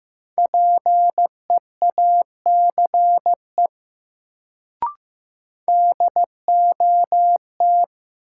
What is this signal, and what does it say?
Morse code, heard as a keyed tone, about 15 wpm, 700 Hz. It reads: PEACE DOT
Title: Sentences from Top 1500 Words - New Word Drill Alt - 15wpm